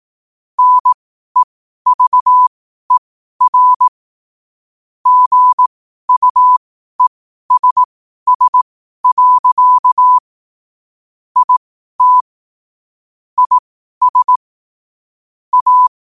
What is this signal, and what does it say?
Morse code, heard as a keyed tone, about 18 wpm, spaced as for 12 wpm, 1000 Hz. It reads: NEVER GUESS. IT IS A
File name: qotdfiles / 12wpm